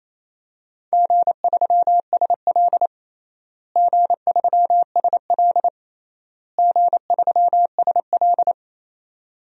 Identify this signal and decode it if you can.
Morse code, heard as a keyed tone, about 28 wpm, 700 Hz. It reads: G3SL G3SL G3SL
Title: DXCC-100 Call Signs - 4 Characters - Repeated 3x - 28wpm